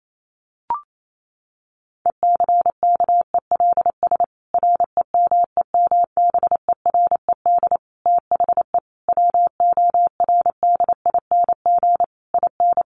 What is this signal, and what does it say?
Morse code, heard as a keyed tone, about 28 wpm, 700 Hz. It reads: ECKELS REMEMBERED THE WORDING IN
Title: A Sound Of Thunder 28wpm